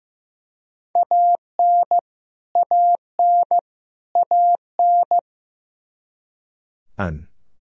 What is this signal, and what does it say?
Morse code, heard as a keyed tone, about 15 wpm, 700 Hz. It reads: AN AN AN
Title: Top 100 Words Repeated 15wpm